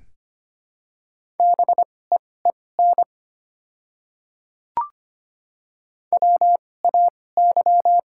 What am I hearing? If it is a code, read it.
Morse code, heard as a keyed tone, about 25 wpm, spaced as for 18 wpm, 700 Hz. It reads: BEEN WAY
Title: 2x Character Spacing - Top 200 Words - 25wpm